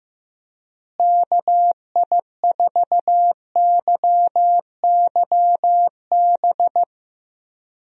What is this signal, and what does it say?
Morse code, heard as a keyed tone, about 15 wpm, 700 Hz. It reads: KI4YYB
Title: USA Call Signs - 6 Characters - Repeated 3x - 15wpm